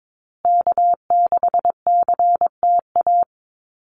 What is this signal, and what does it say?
Morse code, heard as a keyed tone, about 22 wpm, 700 Hz. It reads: K6CTA
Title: Call Signs Repeated 2x - 22wpm